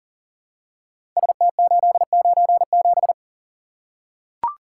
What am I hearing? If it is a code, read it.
Morse code, heard as a keyed tone, about 40 wpm, 700 Hz. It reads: ST897
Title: Code Groups - 5 Letter:Numbers - 40wpm